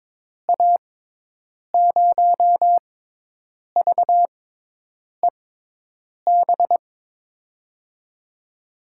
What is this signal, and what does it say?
Morse code, heard as a keyed tone, about 22 wpm, spaced as for 8 wpm, 700 Hz. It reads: A0VEB
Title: USA Call Signs - 6 Characters - 6x Character Spacing - 22wpm